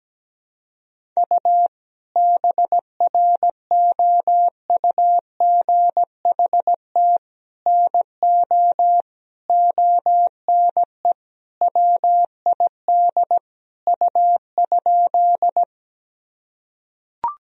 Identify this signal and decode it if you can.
Morse code, heard as a keyed tone, about 17 wpm, 700 Hz. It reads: U BROUGHT NO ONE WID U?
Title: Sentences from Top 500 Words and Abbreviations 17wpm